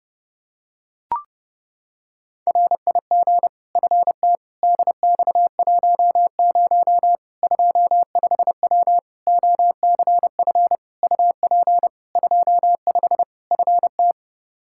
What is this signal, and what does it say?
Morse code, heard as a keyed tone, about 32 wpm, 700 Hz. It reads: RIG FT DX10 25W OCF UP 25 FT
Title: Practice QSO - Exchange 2 - Info - 30wpm